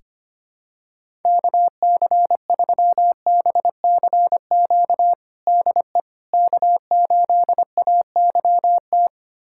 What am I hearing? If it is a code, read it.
Morse code, heard as a keyed tone, about 25 wpm, 700 Hz. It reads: KC3BCQ DE K8AYT
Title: USA Callsign DE Callsign 25wpm